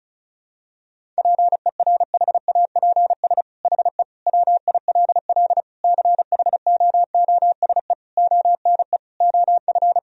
Morse code, heard as a keyed tone, about 36 wpm, 700 Hz. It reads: PERHAPS HE WILL CHOOSE ONE OF
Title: Sentences from Top 1000 Words - Review Part 4 - 35wpm